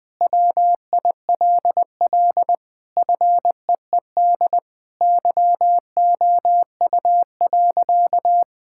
Morse code, heard as a keyed tone, about 21 wpm, 700 Hz. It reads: WILL FEED YOU.
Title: Sentences from Top 1600 Words 20wpm